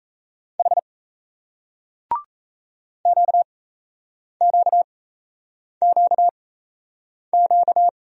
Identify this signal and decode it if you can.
Morse code, heard as a keyed tone, about 37 wpm, spaced as for 8 wpm, 700 Hz. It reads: H QQQQ